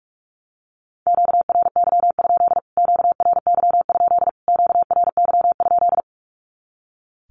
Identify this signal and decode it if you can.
Morse code, heard as a keyed tone, about 44 wpm, 700 Hz. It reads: QRY? QRY? QRY?
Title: Q-Codes Repeated 3x 45wpm